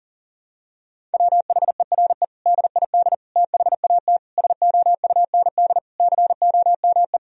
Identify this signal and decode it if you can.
Morse code, heard as a keyed tone, about 40 wpm, 700 Hz. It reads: WHERE DID THAT SOUND COME
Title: Sentences from Top 1000 Words - Review Part 1 - 40wpm